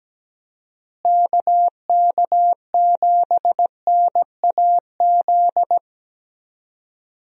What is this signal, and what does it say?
Morse code, heard as a keyed tone, about 17 wpm, 700 Hz. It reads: KK7NAZ
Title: USA Call Signs - 6 Characters - Repeated 3x - 17wpm